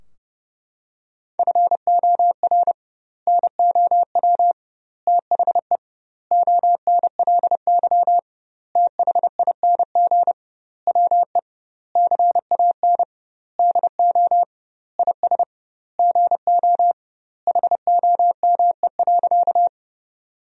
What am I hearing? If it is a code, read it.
Morse code, heard as a keyed tone, about 32 wpm, 700 Hz. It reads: FOR NOW THE ONLY THING WE CAN DO IS GO HOME.
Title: 2x Word-Spacing - Sentences from Top 200 Words - 30wpm